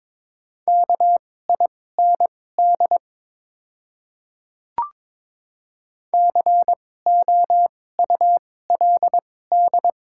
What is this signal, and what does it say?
Morse code, heard as a keyed tone, about 22 wpm, spaced as for 16 wpm, 700 Hz. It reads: KIND COULD
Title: 2x Character Spacing - Top 200 Words - 22wpm